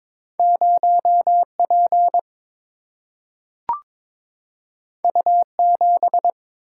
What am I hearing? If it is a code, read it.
Morse code, heard as a keyed tone, about 22 wpm, 700 Hz. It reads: 0 P U 7
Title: Single-Letter Single-Number 22wpm